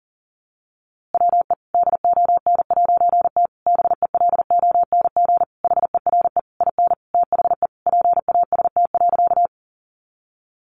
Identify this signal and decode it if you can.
Morse code, heard as a keyed tone, about 40 wpm, 700 Hz. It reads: WE DON'T BELONG HERE IN THE PAST.
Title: A Sound Of Thunder 40wpm